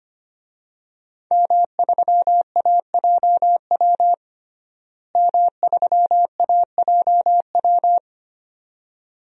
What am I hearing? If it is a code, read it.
Morse code, heard as a keyed tone, about 25 wpm, 700 Hz. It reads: M3AJW M3AJW
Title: DXCC-100 Call Signs - 5 Characters - Repeated 2x - 25wpm